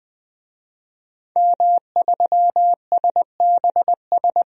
Morse code, heard as a keyed tone, about 20 wpm, 700 Hz. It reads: M3SBS
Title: DXCC-100 Call Signs - 5 Characters - 20wpm